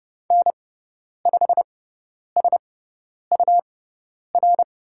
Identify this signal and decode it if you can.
Morse code, heard as a keyed tone, about 30 wpm, spaced as for 10 wpm, 700 Hz. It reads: N5SUR
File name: Lesson 18g - Letter m - Call Signs 30wpm